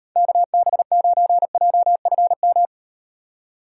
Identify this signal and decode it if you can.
Morse code, heard as a keyed tone, about 38 wpm, 700 Hz. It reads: KB9JFM